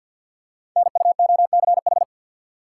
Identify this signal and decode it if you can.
Morse code, heard as a keyed tone, about 50 wpm, 690 Hz. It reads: NUOCH
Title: Code Groups - 5 Letters - 50wpm